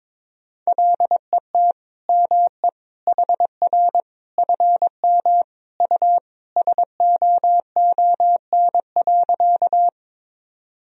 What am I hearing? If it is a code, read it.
Morse code, heard as a keyed tone, about 22 wpm, 700 Hz. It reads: LET ME HR FM U SOON.
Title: Sentences from Top 400 Words and Abbreviations 22wpm